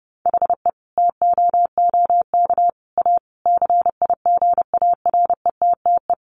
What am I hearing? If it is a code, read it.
Morse code, heard as a keyed tone, about 30 wpm, 700 Hz. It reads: HE TOOK A CIGARETTE
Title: Sentences from Top 1000 Words 30wpm